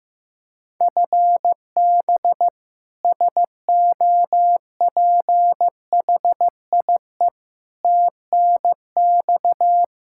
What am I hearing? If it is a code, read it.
Morse code, heard as a keyed tone, about 16 wpm, 700 Hz. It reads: FB SOPHIE TNX
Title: K1USN SST QSO Practice 15wpm